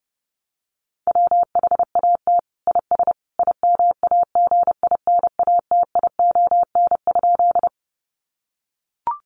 Morse code, heard as a keyed tone, about 30 wpm, 700 Hz. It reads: WHAT IS IMAGINATION?
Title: Sentences from Top 1300 Words 30wpm